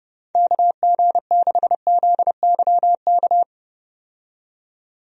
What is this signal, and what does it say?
Morse code, heard as a keyed tone, about 30 wpm, 700 Hz. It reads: KG6ZYK
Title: USA Call Signs - 6 Characters - 30wpm